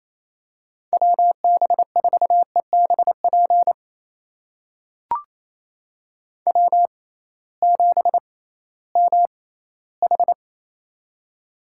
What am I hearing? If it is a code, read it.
Morse code, heard as a keyed tone, about 28 wpm, spaced as for 16 wpm, 700 Hz. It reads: WB4EBP W7MH
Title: USA Call Signs - 6x Character Spacing - 28wpm